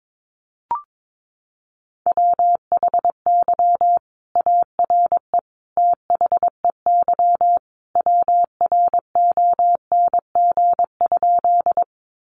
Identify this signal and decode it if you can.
Morse code, heard as a keyed tone, about 22 wpm, 700 Hz. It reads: WHY ARE THEY WRONG?
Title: Sentences from Top 900 Words 22wpm